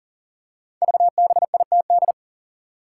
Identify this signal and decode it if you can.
Morse code, heard as a keyed tone, about 38 wpm, 700 Hz. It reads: VBITD
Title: Code Groups - 5 Letters - 40wpm